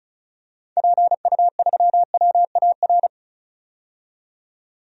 Morse code, heard as a keyed tone, about 34 wpm, 700 Hz. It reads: PU3WAR
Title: DXCC-100 Call Signs - 6 Characters - 35wpm